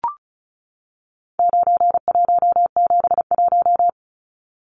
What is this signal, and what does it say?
Morse code, heard as a keyed tone, about 34 wpm, 700 Hz. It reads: 9171
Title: Numbers - 4-Digit 35wpm